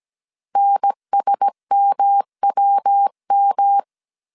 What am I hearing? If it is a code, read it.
Morse code, heard as a keyed tone, about 17 wpm, 800 Hz. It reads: NSMWM
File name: 16wpm